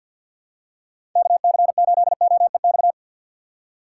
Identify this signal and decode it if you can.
Morse code, heard as a keyed tone, about 47 wpm, 700 Hz. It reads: KC8OEX
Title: USA Call Signs - 2x Character Spacing - 50wpm